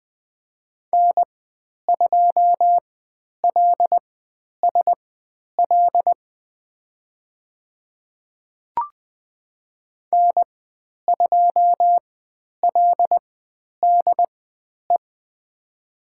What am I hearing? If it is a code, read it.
Morse code, heard as a keyed tone, about 20 wpm, spaced as for 10 wpm, 700 Hz. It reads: N2LSL N2LDE
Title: 109 Lesson 18g - Letter m - Call Signs 20wpm